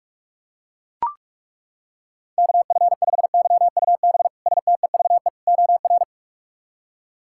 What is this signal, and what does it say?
Morse code, heard as a keyed tone, about 41 wpm, 700 Hz. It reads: KF5YUB STEVE OR
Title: K1USN SST QSO Practice - 45wpm